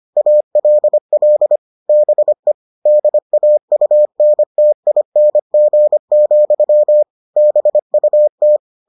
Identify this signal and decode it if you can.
Morse code, heard as a keyed tone, about 25 wpm, 600 Hz. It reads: ALL BE DAUNTING, BUT